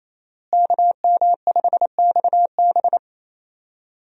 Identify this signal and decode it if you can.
Morse code, heard as a keyed tone, about 28 wpm, 700 Hz. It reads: KM5XB
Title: USA Call Signs - 5 Characters - Repeated 3x - 28wpm